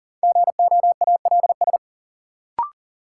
Morse code, heard as a keyed tone, about 40 wpm, 700 Hz. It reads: GOALS
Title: Sentences from Top 1000 Words - Review Part 6 - New Word Drill - 40wpm